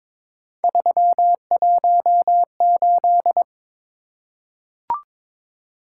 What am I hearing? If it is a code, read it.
Morse code, heard as a keyed tone, about 23 wpm, 700 Hz. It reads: 318
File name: practice / 22wpm / Three Letter-Number Combinations - 22wpm